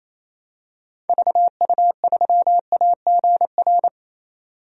Morse code, heard as a keyed tone, about 28 wpm, 700 Hz. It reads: VU3AGR
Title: DXCC-100 Call Signs - 6 Characters - 28wpm